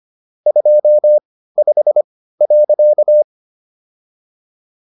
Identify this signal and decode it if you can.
Morse code, heard as a keyed tone, about 25 wpm, spaced as for 15 wpm, 600 Hz. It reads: 25.